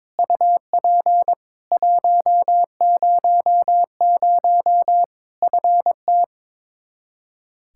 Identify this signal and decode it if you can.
Morse code, heard as a keyed tone, about 22 wpm, 700 Hz. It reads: UP 100 FT
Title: Practice QSO - Exchange 2 - Info - 22wpm